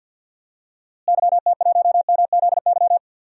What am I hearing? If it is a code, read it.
Morse code, heard as a keyed tone, about 51 wpm, 700 Hz. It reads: YT1MZY
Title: DXCC-100 Call Signs - 6 Characters - Repeated 2x - 50wpm